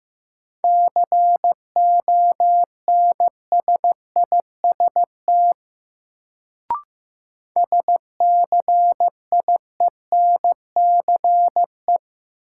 Morse code, heard as a keyed tone, about 15 wpm, 700 Hz. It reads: CONSIST SCIENCE
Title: Sentences from Top 1400 Words - New Word Drill Alt - 15wpm